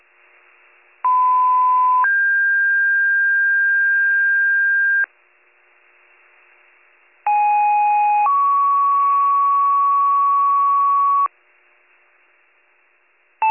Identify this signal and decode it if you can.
Two-tone page, heard as a one-second tone, then a three-second tone, about 1010 then 1600 Hz, about 830 then 1120 Hz. It is heard from 1.0 to 11.3 s.